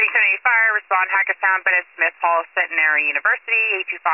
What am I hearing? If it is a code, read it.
Emergency Alert System (SAME) tones, heard as harsh buzzing bursts, with speech. It is heard from 0.4 to 3.0 s.